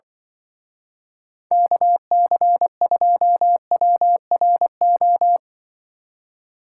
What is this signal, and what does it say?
Morse code, heard as a keyed tone, about 23 wpm, 700 Hz. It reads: KC2WRO
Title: Speed Racing - USA 2x3 Call Signs 20wpm